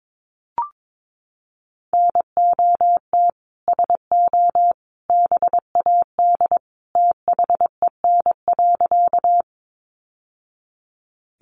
Morse code, heard as a keyed tone, about 22 wpm, 700 Hz. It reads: NOT SO BAD THEN.
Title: Sentences from Top 700 Words 22wpm